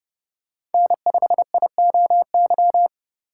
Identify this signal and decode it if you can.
Morse code, heard as a keyed tone, about 30 wpm, 700 Hz. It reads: N5IOY
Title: USA Call Signs - 5 Characters - 2x Character Spacing - 30wpm